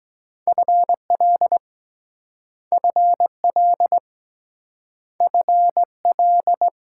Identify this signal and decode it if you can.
Morse code, heard as a keyed tone, about 21 wpm, 700 Hz. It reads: FL FL FL
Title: Speed-Racing Multi-letter Phonemes - 17wpm